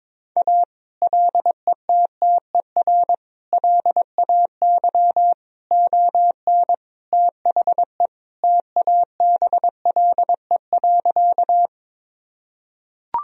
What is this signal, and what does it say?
Morse code, heard as a keyed tone, about 21 wpm, 700 Hz. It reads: A LETTER LAY ON THE TABLE.
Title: Sentences from Top 1000 Words - Review Part 2 - 22wpm